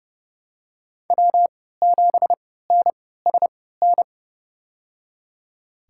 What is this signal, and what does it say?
Morse code, heard as a keyed tone, about 30 wpm, spaced as for 17 wpm, 700 Hz. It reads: W7NSN
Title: USA Call Signs - 5 Characters - 3x Character Spacing - 30wpm